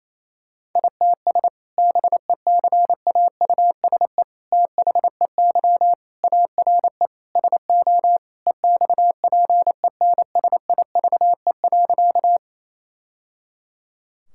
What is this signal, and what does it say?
Morse code, heard as a keyed tone, about 28 wpm, 700 Hz. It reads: ITS BECAUSE THEY ARE SO EXPENSIVE.
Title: Sentences from Top 1000 Words - Review Part 6 - 28wpm